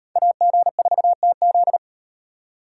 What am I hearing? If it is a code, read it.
Morse code, heard as a keyed tone, about 38 wpm, 700 Hz. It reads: AG4TZ